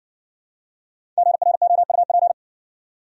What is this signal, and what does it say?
Morse code, heard as a keyed tone, about 60 wpm, 700 Hz. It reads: GROUP